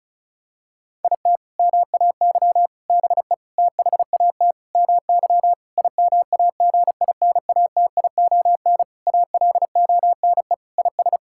Morse code, heard as a keyed tone, about 36 wpm, 700 Hz. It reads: IT MAY BE THAT MY IMAGINATION ALONE IS
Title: Sentences from Top 1600 Words 35wpm